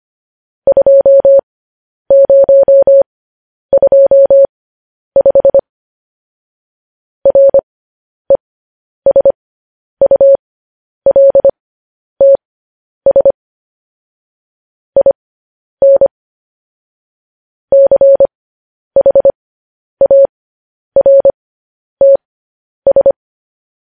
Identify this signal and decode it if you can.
Morse code, heard as a keyed tone, about 25 wpm, spaced as for 10 wpm, 550 Hz. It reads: 2025 RESULTS IN CHARTS
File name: podcast / News headlines-1x-25x10